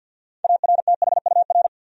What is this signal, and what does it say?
Morse code, heard as a keyed tone, about 48 wpm, 700 Hz. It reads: ARTHUR
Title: USA Common Names - 50wpm